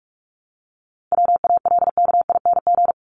Morse code, heard as a keyed tone, about 44 wpm, 700 Hz. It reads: WALKING
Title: ING Suffix - Top 100 Words with Base Words 45wpm